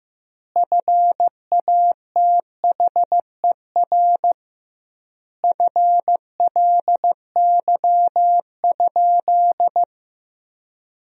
Morse code, heard as a keyed tone, about 15 wpm, 700 Hz. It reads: FATHER FLY?
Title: 2x Word-Spacing - Sentences from Top 500 Words - 15wpm